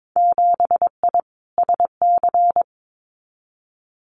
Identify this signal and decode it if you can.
Morse code, heard as a keyed tone, about 22 wpm, 700 Hz. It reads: 7I SC
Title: Winter Field Day Exchanges Encore 22wpm